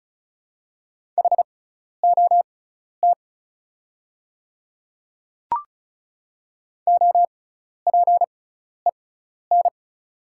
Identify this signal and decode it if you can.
Morse code, heard as a keyed tone, about 36 wpm, spaced as for 12 wpm, 700 Hz. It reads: HOT OPEN
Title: 6x Character Spacing - Top 300 Words - 35wpm